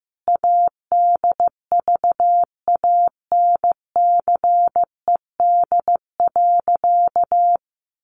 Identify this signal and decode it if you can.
Morse code, heard as a keyed tone, about 14 wpm, 700 Hz. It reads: ADVANCED.
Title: Sentences from Top 1200 Words 15wpm